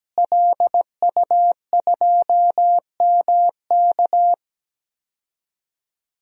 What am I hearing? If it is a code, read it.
Morse code, heard as a keyed tone, about 17 wpm, 700 Hz. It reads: LU2MK
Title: DXCC-100 Call Signs - 5 Characters - Repeated 3x - 17wpm